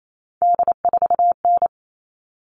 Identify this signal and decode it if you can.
Morse code, heard as a keyed tone, about 28 wpm, 700 Hz. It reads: D4N
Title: All Cut Numbers - 3-Digit 28wpm